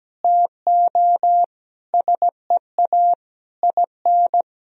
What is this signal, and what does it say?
Morse code, heard as a keyed tone, about 17 wpm, 700 Hz. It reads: TO SEA IN
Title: Sentences from Top 1000 Words - Review Part 3 - 17wpm